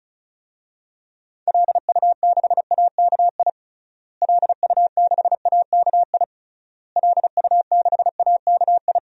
Morse code, heard as a keyed tone, about 36 wpm, 700 Hz. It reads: LU6AKI LU6AKI LU6AKI
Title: DXCC-100 Call Signs - 6 Characters - Repeated 3x - 35wpm